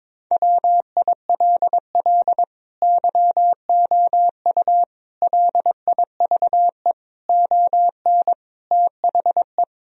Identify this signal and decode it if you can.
Morse code, heard as a keyed tone, about 22 wpm, 700 Hz. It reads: WILL YOU LIVE ON THE
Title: Sentences from Top 1000 Words - Review Part 3 - 22wpm